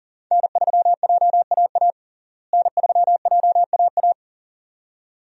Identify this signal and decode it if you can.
Morse code, heard as a keyed tone, about 40 wpm, 700 Hz. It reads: N3JAA N3JAA
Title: USA Call Signs - 5 Characters - Repeated 2x - 40wpm